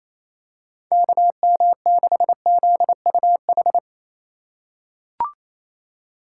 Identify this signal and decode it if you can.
Morse code, heard as a keyed tone, about 28 wpm, 700 Hz. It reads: KM6ZUH
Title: USA Call Signs - 6 Characters - 4x Character Spacing - 28wpm